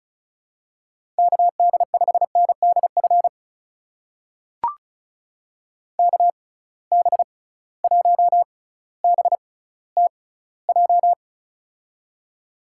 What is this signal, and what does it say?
Morse code, heard as a keyed tone, about 36 wpm, spaced as for 18 wpm, 700 Hz. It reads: KD5NDF KB1BTJ
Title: USA Call Signs - 6 Characters - 6x Character Spacing - 35wpm